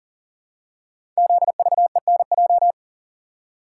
Morse code, heard as a keyed tone, about 42 wpm, 700 Hz. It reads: ZVENJ
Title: Code Groups - 5 Letters - 40wpm